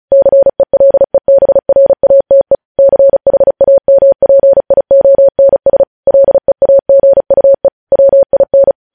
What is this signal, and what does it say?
Morse code, heard as a keyed tone, about 36 wpm, 550 Hz. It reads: CELEBRATE CHAMPIONS LEAGUE WIN
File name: podcast / News headlines-1x-35x35